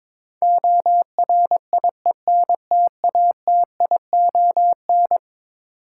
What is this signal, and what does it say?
Morse code, heard as a keyed tone, about 22 wpm, 700 Hz. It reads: ORIENTATION
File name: practice / 22wpm / Eleven Letter Words - 22wpm